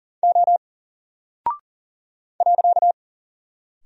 Morse code, heard as a keyed tone, about 40 wpm, 700 Hz. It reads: O .
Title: Single Character (Letter:Number:Punctuation:Prosign) Alt 40wpm